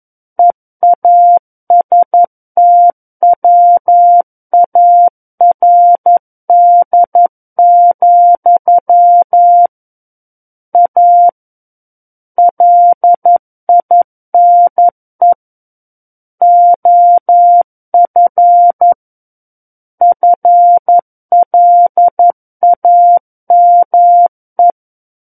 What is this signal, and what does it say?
Morse code, heard as a keyed tone, about 11 wpm, 700 Hz. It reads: EASTWARD, A LINE OF FLAME
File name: 11 WPM